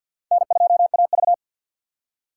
Morse code, heard as a keyed tone, about 62 wpm, 700 Hz. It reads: N2AV
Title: Speed Racing - USA 1x2 Call Signs 50wpm